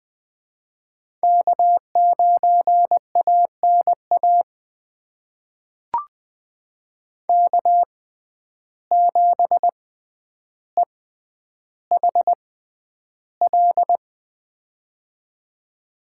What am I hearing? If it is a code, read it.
Morse code, heard as a keyed tone, about 20 wpm, spaced as for 10 wpm, 700 Hz. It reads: K9ANA K7EHL
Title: USA Call Signs - 5 Characters - 6x Character Spacing - 20wpm